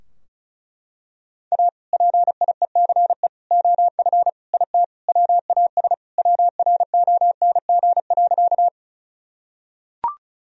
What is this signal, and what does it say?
Morse code, heard as a keyed tone, about 36 wpm, 700 Hz. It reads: A PIECE OF IT WAS WRONG.
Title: Sentences from Top 1000 Words - Review Part 2 - 35wpm